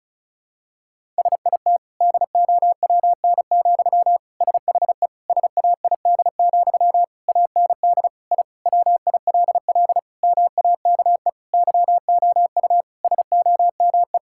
Morse code, heard as a keyed tone, about 36 wpm, 700 Hz. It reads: SIT DOWN, SHE SAID, AND I WILL MAKE YOU SOME
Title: Sentences from Top 1000 Words - Review Part 6 - 35wpm